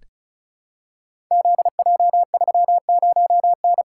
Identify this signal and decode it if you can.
Morse code, heard as a keyed tone, about 36 wpm, 700 Hz. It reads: ZJ30N
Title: Code Groups - 5 Letter:Numbers - 35wpm